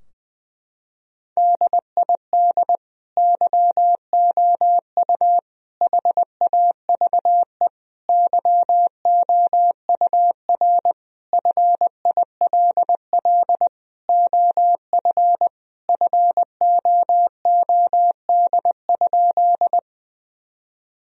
Morse code, heard as a keyed tone, about 20 wpm, 700 Hz. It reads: DID YOU HAVE YOUR FILL OF FOOD?
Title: Sentences from Top 1000 Words - Review Part 3 - 20wpm